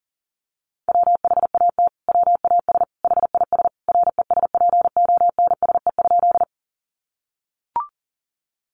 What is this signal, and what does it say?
Morse code, heard as a keyed tone, about 40 wpm, 700 Hz. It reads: WHAT WAS HIS RESPONSE?
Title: Sentences from Top 1400 Words 40wpm